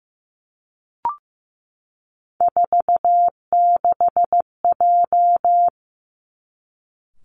Morse code, heard as a keyed tone, about 15 wpm, 700 Hz. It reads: 46J
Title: Three Challenging Letter-Number Combinations 15wpm